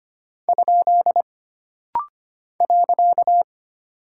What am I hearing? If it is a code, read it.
Morse code, heard as a keyed tone, about 25 wpm, 700 Hz. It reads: ? .
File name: Single Punctuation-Prosign - Rapid-Fire - 25wpm